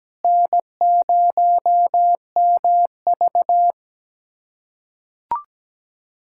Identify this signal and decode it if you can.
Morse code, heard as a keyed tone, about 17 wpm, 700 Hz. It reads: N0MV
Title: USA Callsign DE Callsign - 4 Characters - 17wpm